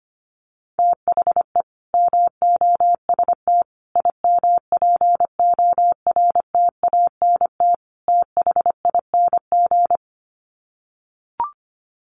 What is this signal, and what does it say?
Morse code, heard as a keyed tone, about 25 wpm, 700 Hz. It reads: THE MOST IMPORTANT THING
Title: Sets of 4 Words 25wpm